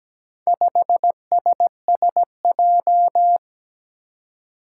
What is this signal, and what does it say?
Morse code, heard as a keyed tone, about 17 wpm, 700 Hz. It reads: 5SSJ
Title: Four Challenging Letter-Number Combinations 17wpm